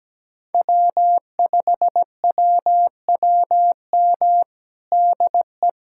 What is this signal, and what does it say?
Morse code, heard as a keyed tone, about 17 wpm, 700 Hz. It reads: W5WWM DE
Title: USA Callsign DE Callsign - 5 Characters - 17wpm